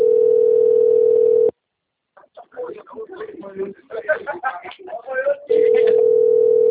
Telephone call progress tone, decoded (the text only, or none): ringback tone